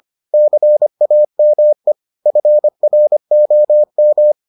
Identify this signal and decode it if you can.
Morse code, heard as a keyed tone, about 25 wpm, 600 Hz. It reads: CAME FROM